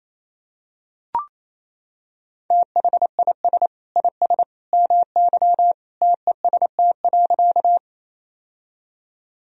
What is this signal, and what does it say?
Morse code, heard as a keyed tone, about 27 wpm, 690 Hz. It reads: THIS IS MY TEST.
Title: Sentences from Top 1000 Words - Review Part 3 - 28wpm